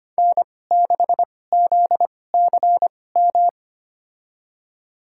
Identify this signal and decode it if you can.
Morse code, heard as a keyed tone, about 25 wpm, spaced as for 18 wpm, 700 Hz. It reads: N6ZCM
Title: USA Call Signs - 5 Characters - 2x Character Spacing - 25wpm